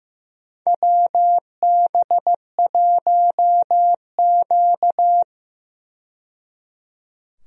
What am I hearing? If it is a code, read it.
Morse code, heard as a keyed tone, about 16 wpm, 700 Hz. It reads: WB1Q
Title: USA Call Signs - 4 Characters - 15wpm